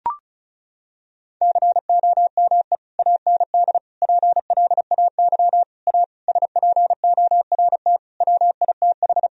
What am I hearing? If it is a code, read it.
Morse code, heard as a keyed tone, about 36 wpm, 700 Hz. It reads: COME AND PLAY A SPORT WITH
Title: Sentences from Top 1000 Words - Review Part 4 - 35wpm